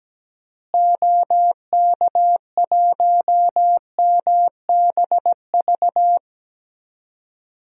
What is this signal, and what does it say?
Morse code, heard as a keyed tone, about 17 wpm, 700 Hz. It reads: OK1MBV
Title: DXCC-100 Call Signs - 6 Characters - Repeated 3x - 17wpm